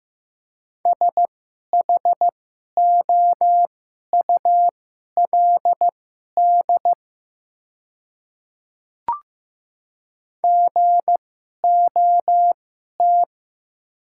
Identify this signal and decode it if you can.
Morse code, heard as a keyed tone, about 16 wpm, spaced as for 11 wpm, 700 Hz. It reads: SHOULD GOT
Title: 2x Character Spacing - Top 300 Words - 15wpm